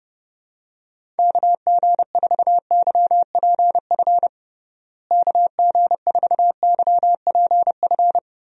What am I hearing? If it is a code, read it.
Morse code, heard as a keyed tone, about 32 wpm, 700 Hz. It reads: KG4YPF KG4YPF
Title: USA Call Signs - 6 Characters - Repeated 3x - 30wpm